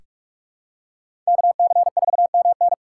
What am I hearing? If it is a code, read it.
Morse code, heard as a keyed tone, about 46 wpm, 700 Hz. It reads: KC4MN
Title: USA Call Signs - 5 Characters - 3x Character Spacing - 45wpm